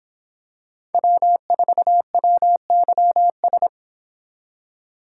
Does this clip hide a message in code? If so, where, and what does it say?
Yes: Morse code from 0.9 to 3.7 s, heard as a keyed tone, about 26 wpm, 700 Hz. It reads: W4WYS